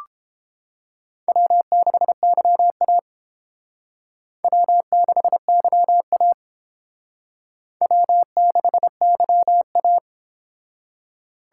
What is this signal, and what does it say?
Morse code, heard as a keyed tone, about 30 wpm, 700 Hz. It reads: W6YA W6YA W6YA